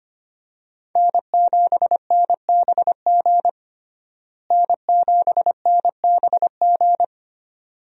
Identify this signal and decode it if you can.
Morse code, heard as a keyed tone, about 25 wpm, 700 Hz. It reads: N7NBG N7NBG
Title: USA Call Signs - 5 Characters - Repeated 2x - 25wpm